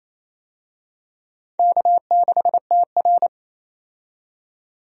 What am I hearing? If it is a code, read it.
Morse code, heard as a keyed tone, about 28 wpm, 700 Hz. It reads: K6TR
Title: USA Call Signs - 4 Characters - 28wpm